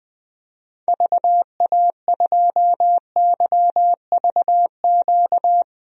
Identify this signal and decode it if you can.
Morse code, heard as a keyed tone, about 20 wpm, 700 Hz. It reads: VA2YVQ
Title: DXCC-100 Call Signs - 6 Characters - 20wpm